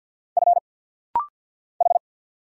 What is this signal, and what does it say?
Morse code, heard as a keyed tone, about 50 wpm, 700 Hz. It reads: F H
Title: Single Letter-Number - Rapid-Fire 50wpm